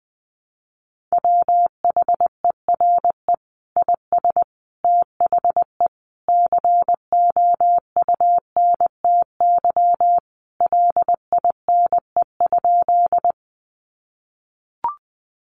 Morse code, heard as a keyed tone, about 19 wpm, 700 Hz. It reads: WHERE IS THE COUNTY LINE?
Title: Sentences from Top 1200 Words 20wpm